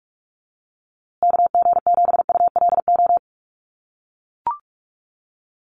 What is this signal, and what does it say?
Morse code, heard as a keyed tone, about 44 wpm, 690 Hz. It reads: KG7ULO
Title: USA Call Signs Repeated 3x 45wpm